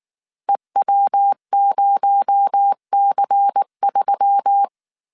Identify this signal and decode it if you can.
Morse code, heard as a keyed tone, about 19 wpm, 800 Hz. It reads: EW0C3
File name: CW / DET / 1 6 1500C S A 18wpm 2255